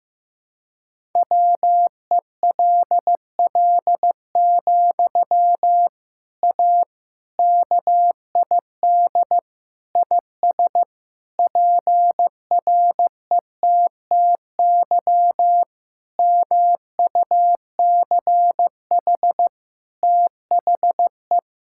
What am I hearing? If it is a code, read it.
Morse code, heard as a keyed tone, about 16 wpm, 700 Hz. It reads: WELL, A KID IS PRETTY MUCH THE
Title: Sentences from Top 1600 Words 15wpm